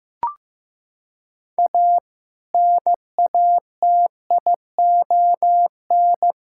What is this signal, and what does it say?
Morse code, heard as a keyed tone, about 15 wpm, 700 Hz. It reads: A NATION
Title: Sentences from Top 1000 Words - Review Part 6 - 15wpm